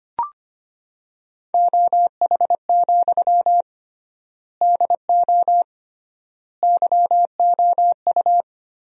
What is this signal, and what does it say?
Morse code, heard as a keyed tone, about 25 wpm, 700 Hz. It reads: OH, DO YOU
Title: 3x Word-Spacing - Sentences from Top 500 Words - 25wpm